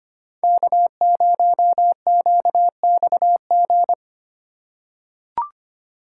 Morse code, heard as a keyed tone, about 25 wpm, 700 Hz. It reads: K0QXG
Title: USA Call Signs - 5 Characters - 2x Character Spacing - 25wpm